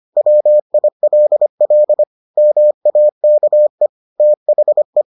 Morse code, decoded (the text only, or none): WILL MAKE THE